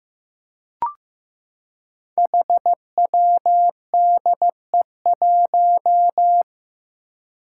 Morse code, heard as a keyed tone, about 15 wpm, 700 Hz. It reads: HWDE1
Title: Five Challenging Letter-Number Combinations 15wpm